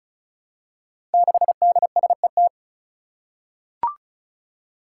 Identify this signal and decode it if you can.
Morse code, heard as a keyed tone, about 34 wpm, 700 Hz. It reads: 6DSET
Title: Five Challenging Letter-Number Combinations 35wpm